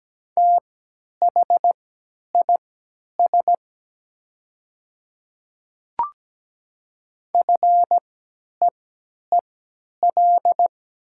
Morse code, heard as a keyed tone, about 17 wpm, spaced as for 10 wpm, 700 Hz. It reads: THIS FEEL
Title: 3x Character Spacing - Top 500 Words - 17wpm